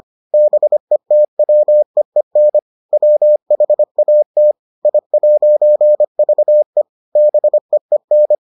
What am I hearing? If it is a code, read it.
Morse code, heard as a keyed tone, about 25 wpm, 600 Hz. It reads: BETWEEN WHAT I'VE BEEN